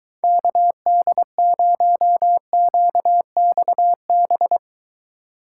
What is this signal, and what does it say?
Morse code, heard as a keyed tone, about 23 wpm, 700 Hz. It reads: KD0QXB